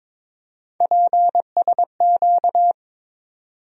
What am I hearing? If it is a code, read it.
Morse code, heard as a keyed tone, about 22 wpm, 700 Hz. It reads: PSQ
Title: USA Call Sign Postfixes - 3 Letters 22wpm